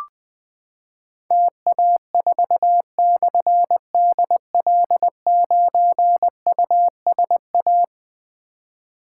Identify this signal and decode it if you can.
Morse code, heard as a keyed tone, about 20 wpm, 700 Hz. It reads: TA4/DL9USA
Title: Difficult Call Signs 20wpm